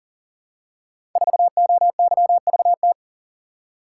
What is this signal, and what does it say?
Morse code, heard as a keyed tone, about 40 wpm, 700 Hz. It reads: 4OYVT
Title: Code Groups - 5 Letter:Numbers - 40wpm